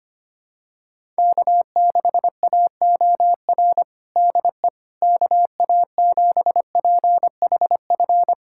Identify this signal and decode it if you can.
Morse code, heard as a keyed tone, about 25 wpm, 700 Hz. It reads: K6AOR DE KA7PHF
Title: USA Callsign DE Callsign 25wpm